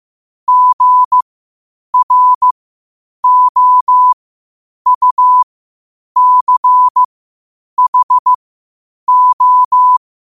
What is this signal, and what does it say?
Morse code, heard as a keyed tone, about 15 wpm, spaced as for 8 wpm, 1000 Hz. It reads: GROUCHO
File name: qotdfiles / 8wpm